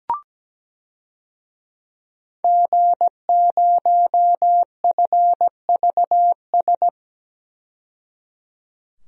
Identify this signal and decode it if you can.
Morse code, heard as a keyed tone, about 17 wpm, 700 Hz. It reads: G0FVS
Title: DXCC-100 Call Signs - 5 Characters - 17wpm